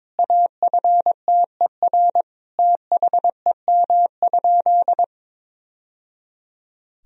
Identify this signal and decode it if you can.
Morse code, heard as a keyed tone, about 22 wpm, 700 Hz. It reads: AFTER THEM?
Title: Sentences from Top 1000 Words - Review Part 1 - 22wpm